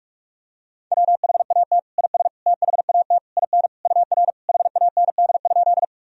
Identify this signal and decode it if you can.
Morse code, heard as a keyed tone, about 46 wpm, 700 Hz. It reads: WHAT IS THAT IN UR HAND?
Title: Sentences from Top 200 Words and Abbreviations 45wpm